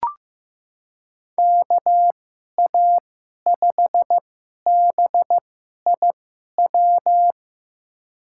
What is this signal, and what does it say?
Morse code, heard as a keyed tone, about 15 wpm, spaced as for 11 wpm, 700 Hz. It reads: KA5BIW
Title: USA Call Signs - 6 Characters - 2x Character Spacing - 15wpm